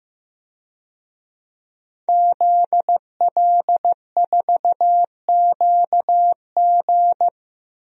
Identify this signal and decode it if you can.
Morse code, heard as a keyed tone, about 15 wpm, 700 Hz. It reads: ZL4QG
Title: DXCC-100 Call Signs - 5 Characters - Repeated 2x - 15wpm